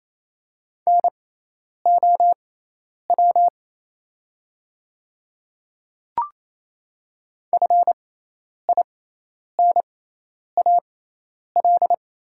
Morse code, heard as a keyed tone, about 28 wpm, spaced as for 10 wpm, 700 Hz. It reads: NOW FINAL
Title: 6x Character Spacing - Top 500 Words - 28wpm